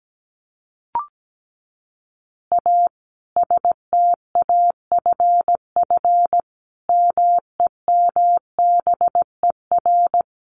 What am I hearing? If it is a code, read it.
Morse code, heard as a keyed tone, about 17 wpm, 700 Hz. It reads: A STAFF MEMBER
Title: Additional Easy Phrases 17wpm